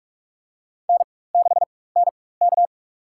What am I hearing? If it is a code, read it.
Morse code, heard as a keyed tone, about 44 wpm, spaced as for 21 wpm, 700 Hz. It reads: N6NK
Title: USA Call Signs - 4 Characters - 4x Character Spacing - 45wpm